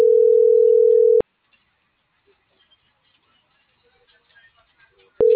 Ringback tone (North America), 440 + 480 Hz.
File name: CENTRAL TERRESTRE / out vane.lazaro - .37616